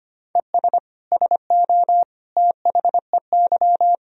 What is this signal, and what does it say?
Morse code, heard as a keyed tone, about 25 wpm, 700 Hz. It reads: ES SO THEY
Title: Sentences from Top 400 Words and Abbreviations 25wpm